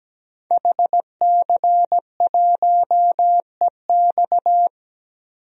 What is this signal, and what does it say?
Morse code, heard as a keyed tone, about 17 wpm, 700 Hz. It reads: HC1EX
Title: DXCC-100 Call Signs - 5 Characters - 17wpm